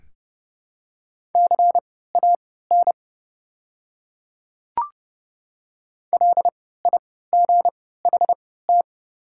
Morse code, heard as a keyed tone, about 29 wpm, spaced as for 17 wpm, 700 Hz. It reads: CAN LIGHT
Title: 3x Character Spacing - Top 200 Words - 30wpm